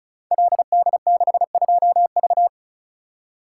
Morse code, heard as a keyed tone, about 36 wpm, 700 Hz. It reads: LD62V
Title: Code Groups - 5 Letter:Numbers - 35wpm